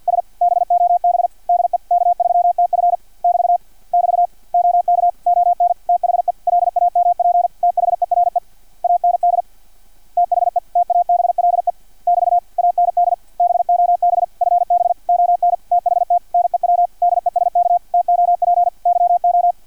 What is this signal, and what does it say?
Morse code, heard as a keyed tone, about 50 wpm, 700 Hz, with radio noise. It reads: S BOB DE K3TF <BT> <BT> OK ON THE LAMP THERE AND THE TABLE <BT> AND BOB FB ON THT NEW DESK TOP YO
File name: archived / CRSnet